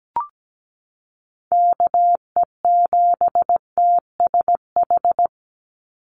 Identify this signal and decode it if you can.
Morse code, heard as a keyed tone, about 17 wpm, 700 Hz. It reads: KE7TSH
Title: USA Call Signs 17wpm